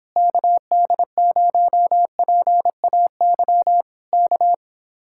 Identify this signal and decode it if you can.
Morse code, heard as a keyed tone, about 26 wpm, 700 Hz. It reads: KD0PAY K